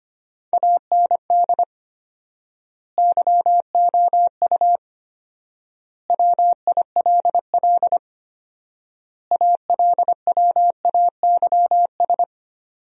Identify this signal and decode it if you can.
Morse code, heard as a keyed tone, about 25 wpm, 700 Hz. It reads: AND YOU WILL ALWAYS
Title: 4x Word-Spacing - Sentences from Top 400 Words - 25wpm